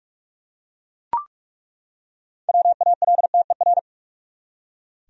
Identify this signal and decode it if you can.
Morse code, heard as a keyed tone, about 41 wpm, 700 Hz. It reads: WALTER
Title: USA Common Names - 45wpm